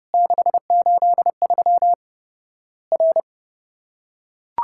Morse code, heard as a keyed tone, about 30 wpm, 700 Hz. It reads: 683 R
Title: Worked All Europe DX Contest - QTC Exchange - 30wpm